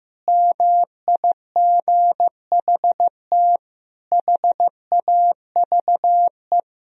Morse code, heard as a keyed tone, about 16 wpm, 700 Hz. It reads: MIGHT HAVE
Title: Sentences from Top 1000 Words - Review Part 1 - 15wpm